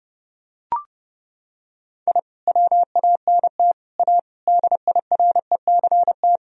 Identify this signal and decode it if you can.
Morse code, heard as a keyed tone, about 30 wpm, 700 Hz. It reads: I WANT A DIRECT
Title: Sentences from Top 1000 Words - Review Part 2 - 30wpm